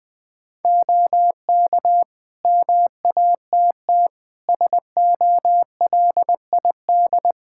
Morse code, heard as a keyed tone, about 20 wpm, 700 Hz. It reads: OK MATT SOLID
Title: Practice QSO - Exchange 3 - More Info - 20wpm